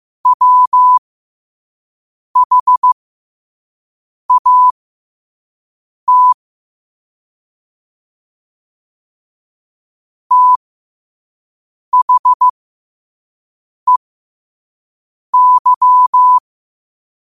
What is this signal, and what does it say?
Morse code, heard as a keyed tone, about 15 wpm, spaced as for 5 wpm, 1000 Hz. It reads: WHAT THEY
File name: qotdfiles / 5wpm